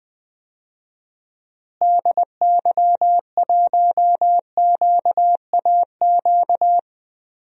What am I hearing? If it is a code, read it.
Morse code, heard as a keyed tone, about 20 wpm, 700 Hz. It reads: DY1QAQ
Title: DXCC-100 Call Signs - 6 Characters - Repeated 2x - 20wpm